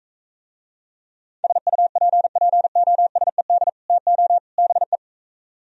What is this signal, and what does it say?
Morse code, heard as a keyed tone, about 45 wpm, 700 Hz. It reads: SUPPOSED TO BE